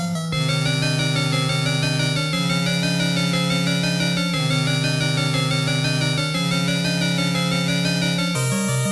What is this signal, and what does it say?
There is no code.